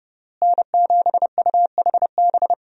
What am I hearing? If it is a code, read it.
Morse code, heard as a keyed tone, about 30 wpm, 700 Hz. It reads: N7UHB
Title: USA Call Signs - 5 Characters - Repeated 3x - 30wpm